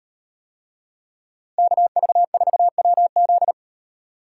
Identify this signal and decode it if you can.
Morse code, heard as a keyed tone, about 36 wpm, 700 Hz. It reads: KV4WZ